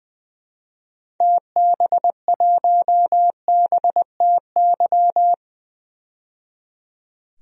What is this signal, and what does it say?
Morse code, heard as a keyed tone, about 20 wpm, 700 Hz. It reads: TB1BTY
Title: DXCC-100 Call Signs - 6 Characters - 20wpm